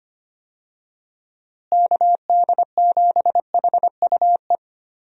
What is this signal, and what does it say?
Morse code, heard as a keyed tone, about 25 wpm, 700 Hz. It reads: KD7HUE
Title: USA Call Signs - 6 Characters - Repeated 3x - 25wpm